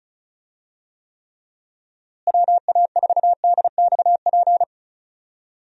Morse code, heard as a keyed tone, about 36 wpm, 700 Hz. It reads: WA4DXP
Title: USA Call Signs - 6 Characters - Repeated 3x - 35wpm